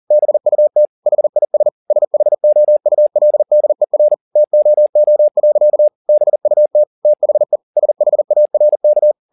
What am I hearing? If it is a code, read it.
Morse code, heard as a keyed tone, about 40 wpm, 600 Hz. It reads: BUT HIS SHOULDER TOO. BUT THE SHARK
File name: the old man and the sea 2 40wpm